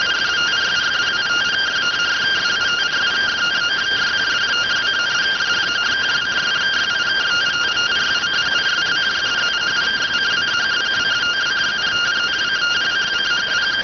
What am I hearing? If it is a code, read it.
Two-tone teleprinter signal, about 170 Hz shift.